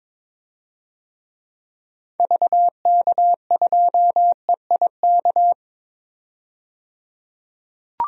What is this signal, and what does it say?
Morse code, heard as a keyed tone, about 22 wpm, 700 Hz. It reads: VK2EIK
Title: DXCC-100 Call Signs - 6 Characters - 22wpm